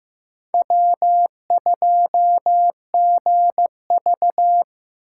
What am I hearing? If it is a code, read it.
Morse code, heard as a keyed tone, about 18 wpm, 700 Hz. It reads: W2GV
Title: Speed Racing - USA Call Signs 15wpm